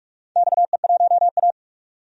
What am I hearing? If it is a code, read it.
Morse code, heard as a keyed tone, about 45 wpm, 700 Hz. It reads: XE1A
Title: DXCC-100 Call Signs - 4 Characters - 45wpm